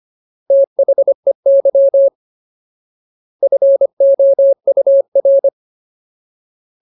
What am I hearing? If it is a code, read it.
Morse code, heard as a keyed tone, about 25 wpm, 550 Hz. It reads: THEY FOUR
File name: Ultimate Word List / 25WPM Ultimate